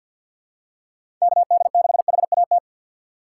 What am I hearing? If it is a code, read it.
Morse code, heard as a keyed tone, about 52 wpm, 700 Hz. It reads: KD6HAT
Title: USA Call Signs - 6x Character Spacing - 50wpm